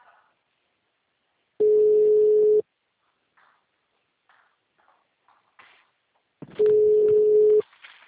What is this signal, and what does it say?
Ringback tone (Europe, 425 Hz).